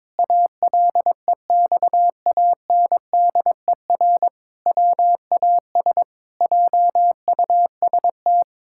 Morse code, heard as a keyed tone, about 22 wpm, 700 Hz. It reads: ALEXANDER WAS JUST
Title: Pangram Challenge v2 - 22wpm